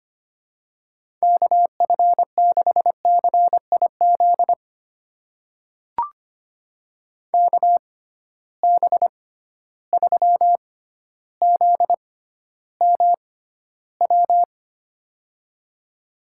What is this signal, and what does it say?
Morse code, heard as a keyed tone, about 25 wpm, spaced as for 13 wpm, 700 Hz. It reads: KF6CIZ KB3ZMW
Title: USA Call Signs - 6 Characters - 6x Character Spacing - 25wpm